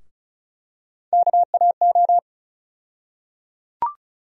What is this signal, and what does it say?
Morse code, heard as a keyed tone, about 36 wpm, 700 Hz. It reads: KAO
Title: USA Call Sign Postfixes - 3 Letters 35wpm